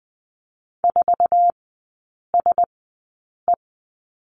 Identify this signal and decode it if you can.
Morse code, heard as a keyed tone, about 20 wpm, spaced as for 8 wpm, 700 Hz. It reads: 4SE
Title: 074 Lesson 13e - Letter L - Three Characters 20wpm